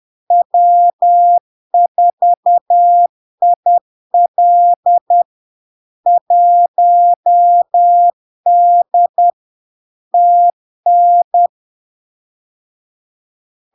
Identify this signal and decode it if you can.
Morse code, heard as a keyed tone, about 10 wpm, 700 Hz. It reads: W4IL 1D TN
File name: LARK-FD-2022-Log-in-CW-10wpm-700-Hz